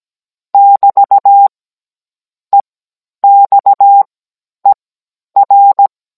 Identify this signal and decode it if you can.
Morse code, heard as a keyed tone, about 17 wpm, spaced as for 10 wpm, 800 Hz. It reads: <BT> EXER
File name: Cours-NR19-16mm